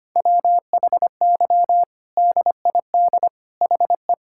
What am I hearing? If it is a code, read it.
Morse code, heard as a keyed tone, about 25 wpm, 700 Hz. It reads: WHY DID HE
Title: Sentences from Top 1600 Words 25wpm